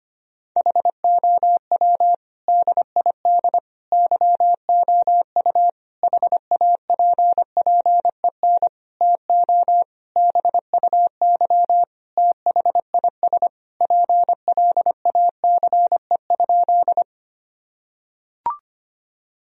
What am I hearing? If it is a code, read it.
Morse code, heard as a keyed tone, about 25 wpm, 700 Hz. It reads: HOW DID YOU HAPPEN TO BUY THIS PLACE?
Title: Sentences from Top 1000 Words - Review Part 4 - 25wpm